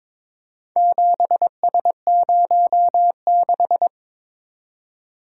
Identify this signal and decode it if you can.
Morse code, heard as a keyed tone, about 23 wpm, 700 Hz. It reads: 7S06
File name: practice / 22wpm / Four Letter-Number Combinations - 22wpm